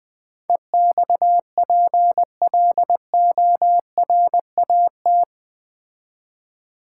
Morse code, heard as a keyed tone, about 20 wpm, 700 Hz. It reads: EXPLORAT
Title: Build A Word - 4 Letters At A Time - 11-Letter Words - 20wpm